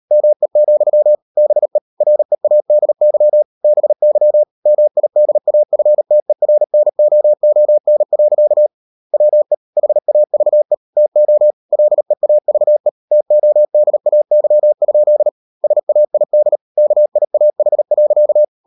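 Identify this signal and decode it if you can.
Morse code, heard as a keyed tone, about 38 wpm, 600 Hz. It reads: ME, BE READY BY MIDAFTERNOON. WE HAVE TO LEAVE TODAY? SAID KIAH.
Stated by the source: Morse code practice at 38 WPM, sending Ken Follett's novel Never in English, from CW med Gnister.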